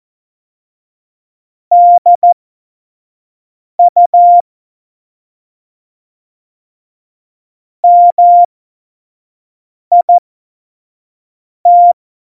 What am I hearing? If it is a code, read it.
Morse code, heard as a keyed tone, about 14 wpm, spaced as for 5 wpm, 700 Hz. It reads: DU MIT
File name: Morsekurs / Witze / Witz 10.Juli 2018, 25 F70